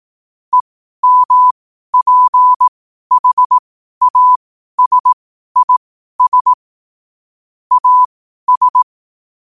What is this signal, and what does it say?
Morse code, heard as a keyed tone, about 18 wpm, spaced as for 12 wpm, 1000 Hz. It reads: EMPHASIS AS
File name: qotdfiles / 12wpm